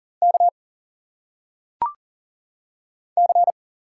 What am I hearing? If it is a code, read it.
Morse code, heard as a keyed tone, about 40 wpm, 700 Hz. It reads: K C
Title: Single Letters - Phonetic Pronunciation 40wpm